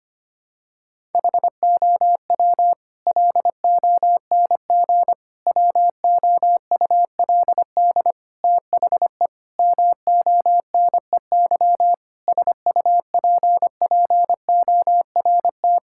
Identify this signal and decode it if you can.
Morse code, heard as a keyed tone, about 25 wpm, 700 Hz. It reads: HOW LONG WOULD THE MONEY SUPPORT
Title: Sentences from Top 1000 Words - Review Part 5 - 25wpm